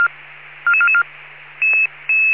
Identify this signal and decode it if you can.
Five-tone selective call (ZVEI), heard as quick runs of five tones, tones of about 80 ms. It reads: 44444
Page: SDR/ZVEI